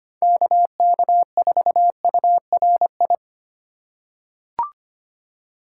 Morse code, heard as a keyed tone, about 25 wpm, 700 Hz. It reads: KK4URI
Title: USA Call Signs - 4x Character Spacing - 25wpm